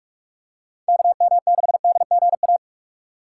Morse code, heard as a keyed tone, about 41 wpm, 700 Hz. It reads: KM6DGA
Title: USA Call Signs - 6x Character Spacing - 45wpm